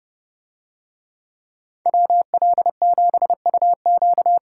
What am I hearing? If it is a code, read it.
Morse code, heard as a keyed tone, about 30 wpm, 700 Hz. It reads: WL7UQ